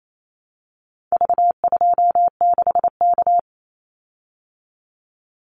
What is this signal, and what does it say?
Morse code, heard as a keyed tone, about 28 wpm, 700 Hz. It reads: V26K
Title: Call Signs Repeated 2x - 28wpm